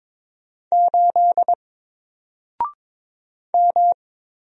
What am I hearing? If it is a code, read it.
Morse code, heard as a keyed tone, about 22 wpm, 700 Hz. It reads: 8 M
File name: Single-Letter-Number-22wpm